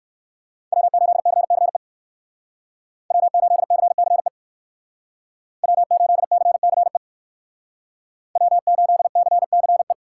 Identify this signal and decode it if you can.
Morse code, heard as a keyed tone, about 62 wpm, 690 Hz. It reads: W8CCE W8CCE W8CCE W8CCE